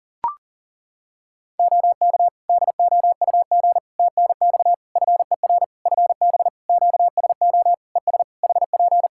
Morse code, heard as a keyed tone, about 40 wpm, 700 Hz. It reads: OK DOUG TNX FER FB QSO ES HP
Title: K1USN SST QSO Practice 40wpm